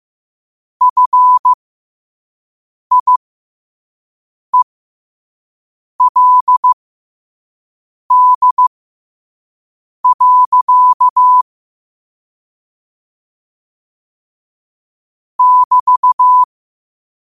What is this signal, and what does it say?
Morse code, heard as a keyed tone, about 15 wpm, spaced as for 5 wpm, 1000 Hz. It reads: FIELD. <BT>
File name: qotdfiles / 5wpm